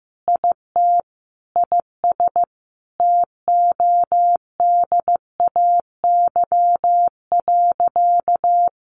Morse code, heard as a keyed tone, about 15 wpm, 700 Hz. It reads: IT IS TODAY.
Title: Sentences from Top 1500 Words 15wpm